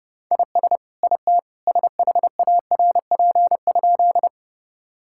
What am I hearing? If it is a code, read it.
Morse code, heard as a keyed tone, about 32 wpm, 700 Hz. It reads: IS IT SHARP?
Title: Sentences from Top 1600 Words 30wpm